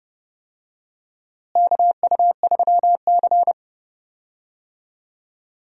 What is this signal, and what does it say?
Morse code, heard as a keyed tone, about 32 wpm, 700 Hz. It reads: KU3C
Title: USA Call Signs - 4 Characters - Repeated 3x - 30wpm